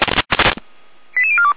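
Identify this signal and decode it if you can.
Five-tone selective call (ZVEI), heard as quick runs of five tones, tones of about 70 ms. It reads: 88042
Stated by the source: Five-tone ZVEI selective call received by SDR, from feuerwehr-orlamuende.org fire brigade audio archive.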